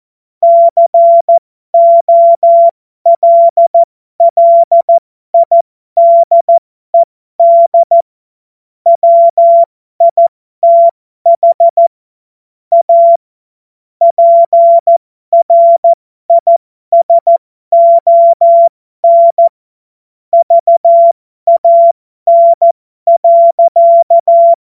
Morse code, heard as a keyed tone, about 14 wpm, spaced as for 12 wpm, 680 Hz. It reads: COLLIDED WITH A PRISON VAN.
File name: Morsekurs / Witze